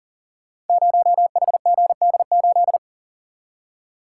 Morse code, heard as a keyed tone, about 40 wpm, 700 Hz. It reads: 0HGD8
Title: Code Groups - 5 Letter:Numbers - 40wpm